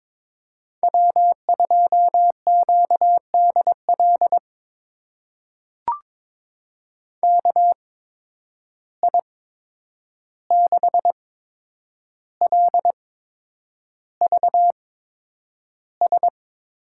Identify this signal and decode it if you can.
Morse code, heard as a keyed tone, about 22 wpm, spaced as for 9 wpm, 700 Hz. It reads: W2QDL KI6LVS
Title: USA Call Signs - 8x Character Spacing - 22wpm